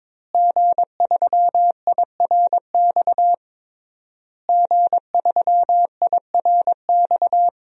Morse code, heard as a keyed tone, about 22 wpm, 700 Hz. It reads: G3IRX G3IRX
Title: DXCC-100 Call Signs - 5 Characters - Repeated 3x - 22wpm